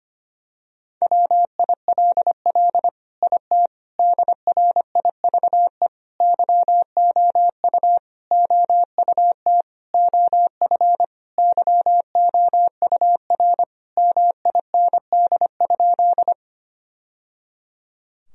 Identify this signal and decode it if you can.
Morse code, heard as a keyed tone, about 25 wpm, 700 Hz. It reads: WILL IT DRIVE YOU OUT OF YOUR MIND?
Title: Sentences from Top 1000 Words - Review Part 2 - 25wpm